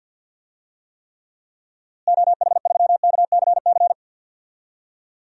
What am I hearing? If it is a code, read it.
Morse code, heard as a keyed tone, about 52 wpm, 700 Hz. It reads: OH3KCC
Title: DXCC-100 Call Signs - 6 Characters - Repeated 3x - 50wpm